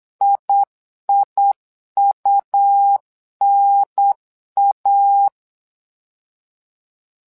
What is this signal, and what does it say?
Morse code, heard as a keyed tone, about 8 wpm, 800 Hz. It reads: IIUNA
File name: LEZIONI / 2 UHDN / 2 UHDN 8wpm2232